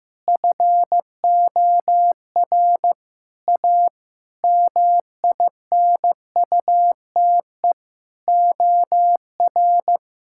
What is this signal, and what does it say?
Morse code, heard as a keyed tone, about 15 wpm, 700 Hz. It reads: FOR A MINUTE OR
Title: Sentences from Top 1600 Words 15wpm